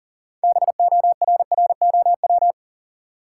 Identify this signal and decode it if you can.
Morse code, heard as a keyed tone, about 40 wpm, 700 Hz. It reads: BORROW
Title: Six Letter Words - 40wpm